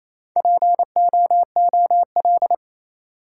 Morse code, heard as a keyed tone, about 27 wpm, 690 Hz. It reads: POOL
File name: Easy Words - Speed-Racing with Hints - 22wpm